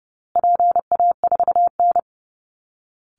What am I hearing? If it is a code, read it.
Morse code, heard as a keyed tone, about 30 wpm, 700 Hz. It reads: PA4N
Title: Call Signs Repeated 2x - 30wpm